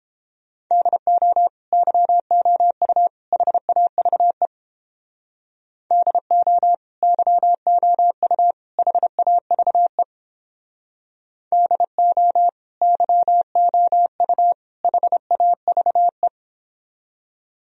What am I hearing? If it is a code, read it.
Morse code, heard as a keyed tone, about 29 wpm, 700 Hz. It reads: DO YOU HAVE DO YOU HAVE DO YOU HAVE